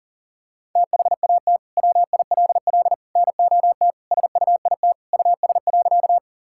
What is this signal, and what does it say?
Morse code, heard as a keyed tone, about 40 wpm, 700 Hz. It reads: THAT WILL NOT SUIT US.
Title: Sentences from Top 1600 Words 40wpm